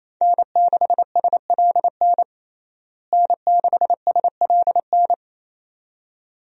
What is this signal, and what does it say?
Morse code, heard as a keyed tone, about 27 wpm, 700 Hz. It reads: N6SLN N6SLN
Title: USA Call Signs - 5 Characters - Repeated 2x - 28wpm